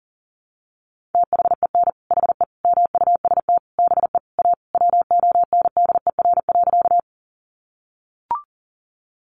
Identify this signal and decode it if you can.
Morse code, heard as a keyed tone, about 40 wpm, 700 Hz. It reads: THEN HE MUST BE A WONDER.
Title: Sentences from Top 500 Words 40wpm